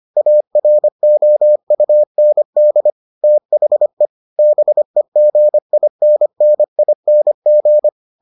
Morse code, heard as a keyed tone, about 25 wpm, 600 Hz. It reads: AROUND THE BEGINNING